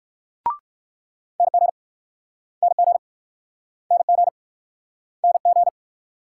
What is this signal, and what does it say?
Morse code, heard as a keyed tone, about 64 wpm, 690 Hz. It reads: NG NG NG NG